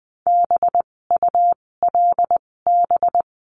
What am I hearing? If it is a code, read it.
Morse code, heard as a keyed tone, about 20 wpm, spaced as for 16 wpm, 700 Hz. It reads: BULB
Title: 256 Lesson 42a - Words 12-20wpm